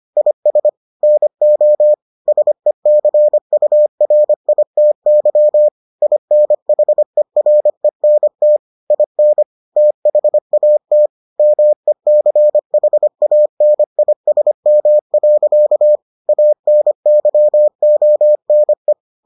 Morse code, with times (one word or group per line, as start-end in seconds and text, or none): 0.2-0.7 IS
1.0-1.9 NO
2.3-5.7 SECURITY
6.0-8.6 INHERENT
8.9-9.4 IN
9.8-11.1 THAT
11.4-16.0 MECHANISM.
16.3-18.9 ANYONE